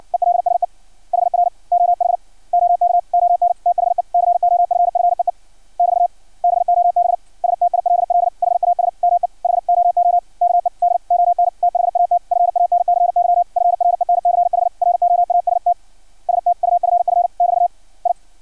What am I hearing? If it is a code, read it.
Morse code, heard as a keyed tone, about 56 wpm, 700 Hz. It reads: ELSE HR OH OK ON THE COFFEE <BT> BOB STILL HAS ME HOOKED ON THAT FANCY FRENCH ROAST STUFF <BT> I
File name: archived / CRSnet